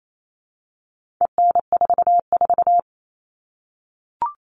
Morse code, heard as a keyed tone, about 28 wpm, 700 Hz. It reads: EN44
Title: All Cut Numbers - 4-Digit 28wpm